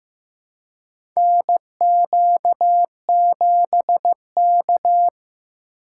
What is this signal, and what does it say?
Morse code, heard as a keyed tone, about 15 wpm, 700 Hz. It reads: NQ7K
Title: USA Call Signs - 4 Characters - 2x Character Spacing - 15wpm